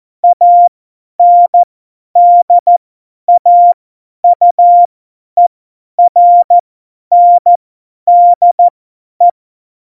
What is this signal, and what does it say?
Morse code, heard as a keyed tone, about 14 wpm, spaced as for 10 wpm, 700 Hz. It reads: ANDAUERNDE
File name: Morsekurs / Witze / Witz 15. September 2020, 50 F70